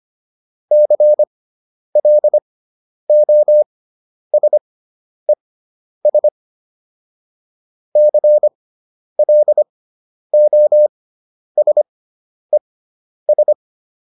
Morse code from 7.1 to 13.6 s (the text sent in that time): CLOSES